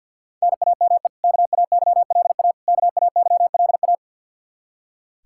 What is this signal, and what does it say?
Morse code, heard as a keyed tone, about 46 wpm, 700 Hz. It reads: NAME KAYLA KAYLA
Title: Practice QSO - Exchange 1 - Beginning - 50wpm